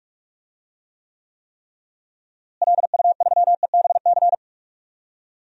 Morse code, heard as a keyed tone, about 43 wpm, 700 Hz. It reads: LU3EBC
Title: DXCC-100 Call Signs - 6 Characters - Repeated 3x - 45wpm